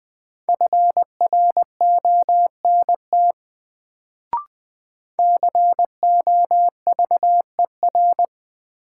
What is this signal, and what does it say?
Morse code, heard as a keyed tone, about 20 wpm, 700 Hz. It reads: FRONT COVER
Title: Sentences from Top 500 Words - New Word Drill - 20wpm